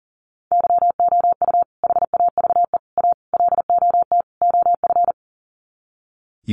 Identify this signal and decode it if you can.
Morse code, heard as a keyed tone, about 40 wpm, 700 Hz. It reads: YOU HAVE A LOT OF
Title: Sets of 5 Words Encore 40wpm